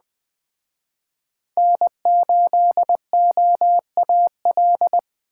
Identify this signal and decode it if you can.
Morse code, heard as a keyed tone, about 21 wpm, 700 Hz. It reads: N8OAL